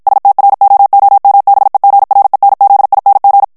Morse code, heard as a keyed tone, about 51 wpm, 800 Hz. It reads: STROOMBEGRENZING
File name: HOLLANDAIS texte clair 2 vitesse 50 mots minute